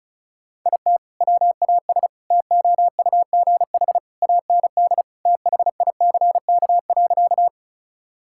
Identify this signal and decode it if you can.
Morse code, heard as a keyed tone, about 34 wpm, 700 Hz. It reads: IT WAS TOUGH AND THICK.
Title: Sentences from Top 1600 Words 35wpm